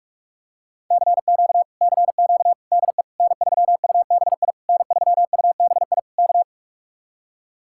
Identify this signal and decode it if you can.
Morse code, heard as a keyed tone, about 44 wpm, 700 Hz. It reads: CQ CQ DE N3UBI N3UBI K